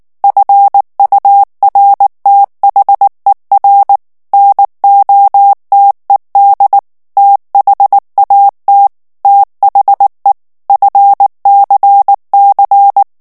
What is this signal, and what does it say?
Morse code, heard as a keyed tone, about 19 wpm, 800 Hz. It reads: FURTHER NOTED THAT THE FCC
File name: ANGLAIS texte clair 3 vitesse 18 mots minute